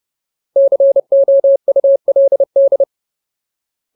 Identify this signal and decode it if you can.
Morse code, heard as a keyed tone, about 30 wpm, 550 Hz. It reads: COULD
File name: Ultimate Word List / 30WPM Ultimate